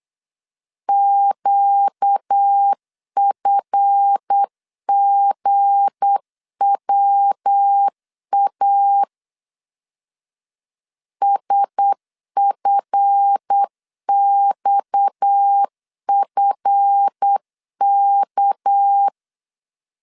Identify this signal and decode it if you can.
Morse code, heard as a keyed tone, about 8 wpm, 800 Hz. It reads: QFGWA SFXFK
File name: LEZIONI / 7 GF9Z / 7 GF 8wpm 3108